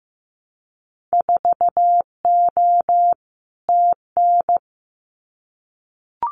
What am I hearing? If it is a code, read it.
Morse code, heard as a keyed tone, about 15 wpm, 700 Hz. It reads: 4O TN
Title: Winter Field Day Exchanges Encore 15wpm